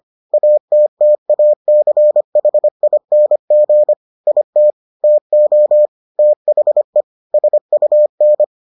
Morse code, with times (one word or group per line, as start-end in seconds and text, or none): 0.3-3.9 ATTACHING
4.3-4.7 IT
5.0-5.9 TO
6.2-7.0 THE
7.3-8.4 SUN